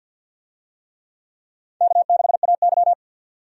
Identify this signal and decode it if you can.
Morse code, heard as a keyed tone, about 50 wpm, 700 Hz. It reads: K6AY
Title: USA Call Signs - 4 Characters - 50wpm